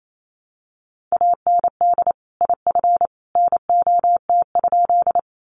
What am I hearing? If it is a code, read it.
Morse code, heard as a keyed tone, about 28 wpm, 700 Hz. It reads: AND IF NOT?
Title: Sentences from Top 600 Words 28wpm